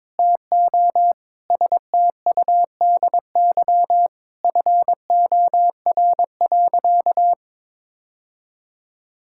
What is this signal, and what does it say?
Morse code, heard as a keyed tone, about 22 wpm, 700 Hz. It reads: TO STUDY FOR.
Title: Sentences from Top 1000 Words - Review Part 5 - 22wpm